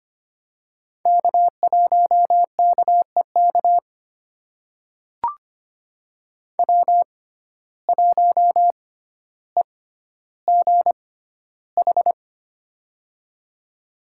Morse code, heard as a keyed tone, about 25 wpm, spaced as for 13 wpm, 700 Hz. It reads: K1KEK W1EGH
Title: USA Call Signs - 5 Characters - 6x Character Spacing - 25wpm